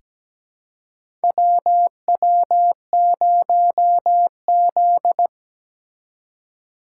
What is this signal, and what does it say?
Morse code, heard as a keyed tone, about 17 wpm, 700 Hz. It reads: WW0Z
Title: USA Call Signs - 3x Character Spacing - 17wpm